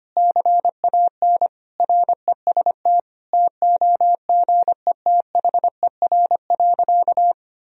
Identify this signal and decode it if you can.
Morse code, heard as a keyed tone, about 25 wpm, 700 Hz. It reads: CAN REST TOGETHER.
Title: Sentences from Top 500 Words and Abbreviations 25wpm